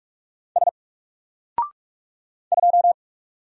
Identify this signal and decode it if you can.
Morse code, heard as a keyed tone, about 44 wpm, 700 Hz. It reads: S 2
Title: Single Letter-Number Alt 45wpm